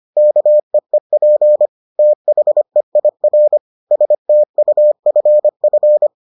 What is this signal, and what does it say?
Morse code, heard as a keyed tone, about 25 wpm, 600 Hz. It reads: KEEP THEIR STUFF